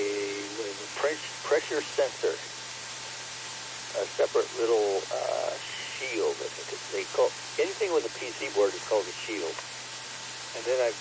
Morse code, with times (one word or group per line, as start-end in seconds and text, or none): none